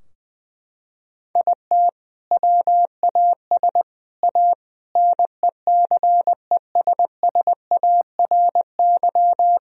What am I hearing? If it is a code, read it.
Morse code, heard as a keyed tone, about 21 wpm, 700 Hz. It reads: IT WAS A NECESSARY
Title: Sentences from Top 1000 Words - Review Part 2 - 20wpm